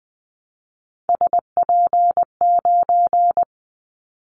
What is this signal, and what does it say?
Morse code, heard as a keyed tone, about 20 wpm, 700 Hz. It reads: SP9
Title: Callsign Prefixes for Top 25 DXCC - Repeated 3x - 20wpm